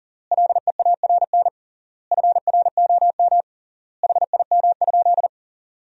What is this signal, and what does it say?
Morse code, heard as a keyed tone, about 40 wpm, 700 Hz. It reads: LEARN FROM HIM?
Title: 3x Word-Spacing - Sentences from Top 300 Words - 40wpm